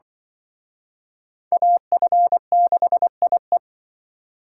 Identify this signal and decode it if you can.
Morse code, heard as a keyed tone, about 25 wpm, 700 Hz. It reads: AF6IE
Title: Speed Racing - USA 2x2 Call Signs 20wpm